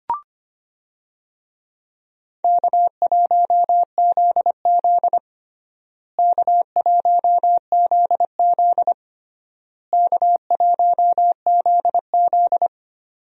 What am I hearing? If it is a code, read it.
Morse code, heard as a keyed tone, about 25 wpm, 700 Hz. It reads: K1ZZ K1ZZ K1ZZ
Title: USA Call Signs - 4 Characters - Repeated 3x - 25wpm